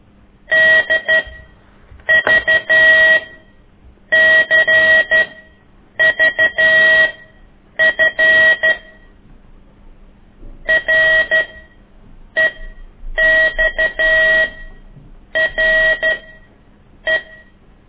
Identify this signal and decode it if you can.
Morse code, heard as a keyed tone, about 12 wpm, spaced as for 7 wpm, 1900 Hz. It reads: DVCVF REXRE
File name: ik1weg2 / CORSI CW